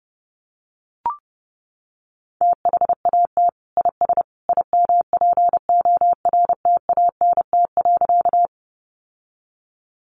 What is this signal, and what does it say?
Morse code, heard as a keyed tone, about 30 wpm, 700 Hz. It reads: THAT IS IMPORTANT.
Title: Sentences from Top 600 Words 30wpm